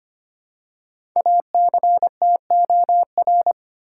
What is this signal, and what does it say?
Morse code, heard as a keyed tone, about 25 wpm, 700 Hz. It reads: ACTOR
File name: 5 Letter Words - First 3 Letters Given - 25wpm